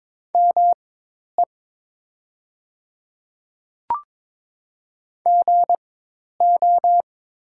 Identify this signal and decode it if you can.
Morse code, heard as a keyed tone, about 22 wpm, spaced as for 10 wpm, 700 Hz. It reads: ME GO
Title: 4x Character Spacing - Top 300 Words - 22wpm